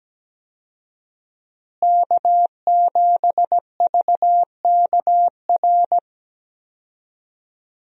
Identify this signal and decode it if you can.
Morse code, heard as a keyed tone, about 17 wpm, 700 Hz. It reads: K7VKR
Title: USA Call Signs - 5 Characters - Repeated 2x - 17wpm